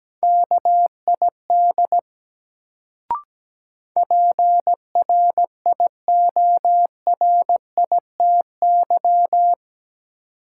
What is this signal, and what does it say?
Morse code, heard as a keyed tone, about 17 wpm, 700 Hz. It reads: KID PRIORITY
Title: Sentences from Top 1600 Words - New Word Drill - 17wpm